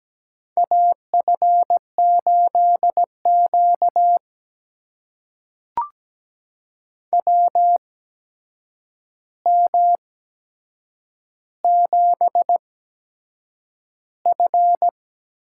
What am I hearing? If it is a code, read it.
Morse code, heard as a keyed tone, about 17 wpm, spaced as for 7 wpm, 700 Hz. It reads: AF8Q WM7F
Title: USA Call Signs - 4 Characters - 8x Character Spacing - 17wpm